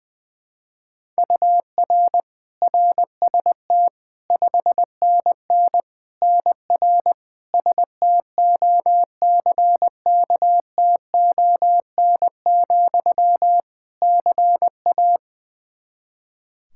Morse code, heard as a keyed tone, about 20 wpm, 700 Hz. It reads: UR RST 5NN NR STOCKTON, CA
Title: Practice QSO - Exchange 1 - Beginning - 20wpm